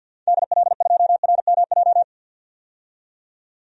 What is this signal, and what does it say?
Morse code, heard as a keyed tone, about 46 wpm, 700 Hz. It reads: DL2RMJ
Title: DXCC-100 Call Signs - 6 Characters - 50wpm